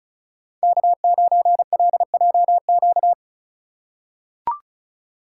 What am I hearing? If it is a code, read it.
Morse code, heard as a keyed tone, about 36 wpm, 700 Hz. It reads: K9LJQ
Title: USA Call Signs - 5 Characters - 2x Character Spacing - 35wpm